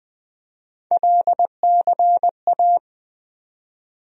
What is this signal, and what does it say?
Morse code, heard as a keyed tone, about 20 wpm, 700 Hz. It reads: LCA
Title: USA Call Sign Postfixes - 3 Letters 20wpm